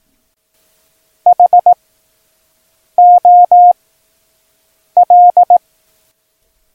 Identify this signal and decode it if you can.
Morse code, heard as a keyed tone, about 18 wpm, 710 Hz. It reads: H O L
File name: acs / Instrument Rating / Instrument ACS